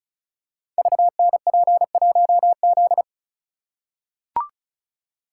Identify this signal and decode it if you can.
Morse code, heard as a keyed tone, about 36 wpm, 700 Hz. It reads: VNP1Z
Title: Code Groups - 5 Letter:Numbers - 35wpm